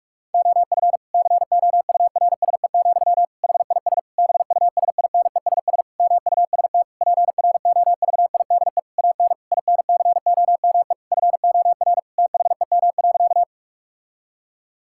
Morse code, heard as a keyed tone, about 41 wpm, 700 Hz. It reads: OF COURSE, HIS BUSINESS MUST PROVIDE AN INCOME FOR THEM.
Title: Sentences from Top 1000 Words - Review Part 5 - 45wpm